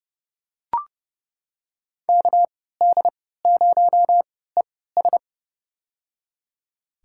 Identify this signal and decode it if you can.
Morse code, heard as a keyed tone, about 32 wpm, spaced as for 17 wpm, 700 Hz. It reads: KD0ES
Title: USA Call Signs - 5 Characters - 3x Character Spacing - 30wpm